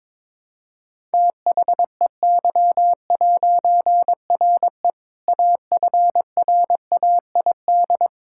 Morse code, heard as a keyed tone, about 22 wpm, 700 Hz. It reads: THEY'RE AFRAID
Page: morse-code-ninja-practice-22wpm